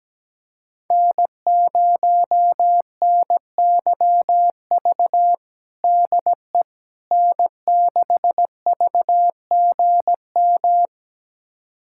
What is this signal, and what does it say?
Morse code, heard as a keyed tone, about 17 wpm, 700 Hz. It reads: N0NYV DE N6VGM
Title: USA Callsign DE Callsign - 5 Characters - 17wpm